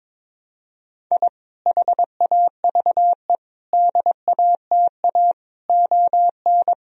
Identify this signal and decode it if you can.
Morse code, heard as a keyed tone, about 22 wpm, 700 Hz. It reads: I HAVE DATA ON
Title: Sentences from Top 1000 Words - Review Part 3 - 22wpm